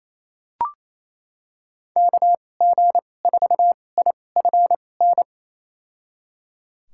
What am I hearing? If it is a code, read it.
Morse code, heard as a keyed tone, about 28 wpm, spaced as for 20 wpm, 700 Hz. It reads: KG4IFN
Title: USA Call Signs - 6 Characters - 2x Character Spacing - 28wpm